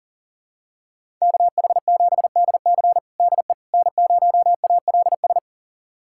Morse code, heard as a keyed tone, about 40 wpm, 700 Hz. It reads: KH7DC DE N0ALS
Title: USA Callsign DE Callsign - 5 Characters - 40wpm